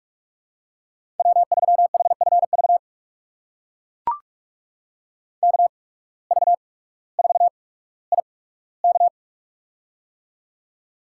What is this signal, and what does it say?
Morse code, heard as a keyed tone, about 44 wpm, spaced as for 19 wpm, 700 Hz. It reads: W3HFV KV4IK
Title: USA Call Signs - 5 Characters - 8x Character Spacing - 45wpm